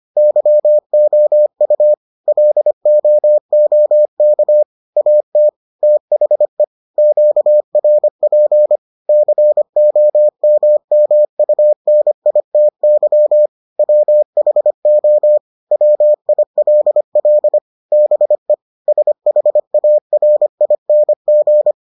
Morse code, heard as a keyed tone, about 25 wpm, 600 Hz. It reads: YOU LOOK AT THE QRP COMMUNITY WHO WILL BE SHARING